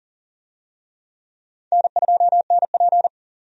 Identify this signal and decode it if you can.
Morse code, heard as a keyed tone, about 40 wpm, 700 Hz. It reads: N2NP
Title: USA Call Signs - 4 Characters - 40wpm